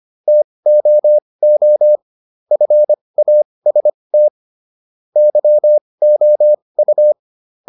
Morse code, heard as a keyed tone, about 25 wpm, spaced as for 20 wpm, 600 Hz. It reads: TOO FAST YOU